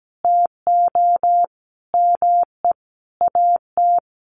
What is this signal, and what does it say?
Morse code, heard as a keyed tone, about 17 wpm, 700 Hz. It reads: TO ME AT
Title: Sentences from Top 500 Words 17wpm